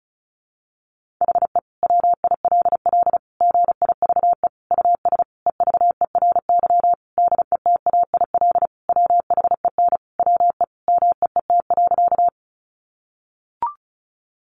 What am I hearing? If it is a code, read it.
Morse code, heard as a keyed tone, about 34 wpm, 700 Hz. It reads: HE WILL GIVE US EVERY DETAIL WHEN WE MEET.
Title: Sentences from Top 1500 Words 35wpm